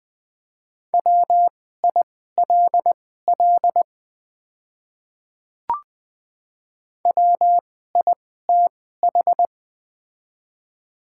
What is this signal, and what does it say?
Morse code, heard as a keyed tone, about 20 wpm, spaced as for 14 wpm, 700 Hz. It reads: WILL WITH
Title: 2x Character Spacing - Top 100 Words - 20wpm